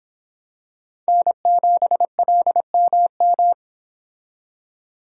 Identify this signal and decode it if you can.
Morse code, heard as a keyed tone, about 25 wpm, 700 Hz. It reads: N7LMM